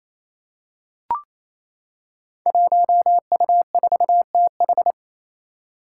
Morse code, heard as a keyed tone, about 27 wpm, 700 Hz. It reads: 1U4TH
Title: Five Challenging Letter-Number Combinations 28wpm